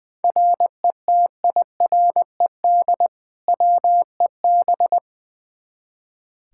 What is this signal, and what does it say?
Morse code, heard as a keyed tone, about 21 wpm, 700 Hz. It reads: RETIRED WEB
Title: K1USN SST QSO Practice 20wpm